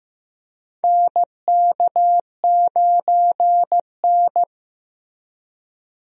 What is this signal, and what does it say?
Morse code, heard as a keyed tone, about 16 wpm, 700 Hz. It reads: NK9N
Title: USA Call Signs - 4 Characters - Repeated 3x - 15wpm